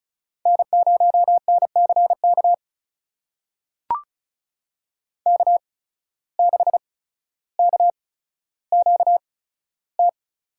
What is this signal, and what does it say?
Morse code, heard as a keyed tone, about 36 wpm, spaced as for 15 wpm, 700 Hz. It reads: N0NCK K6KQT
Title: USA Call Signs - 8x Character Spacing - 35wpm